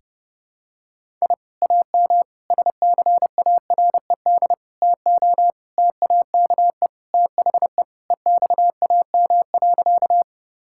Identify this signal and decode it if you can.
Morse code, heard as a keyed tone, about 32 wpm, 700 Hz. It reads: I AM SCARED TO TAKE THE EXAM.
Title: Sentences from Top 1000 Words - Review Part 5 - 30wpm